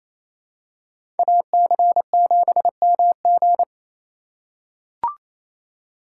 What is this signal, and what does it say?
Morse code, heard as a keyed tone, about 28 wpm, 700 Hz. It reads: AC7MG
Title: USA Call Signs - 2x Character Spacing - 28wpm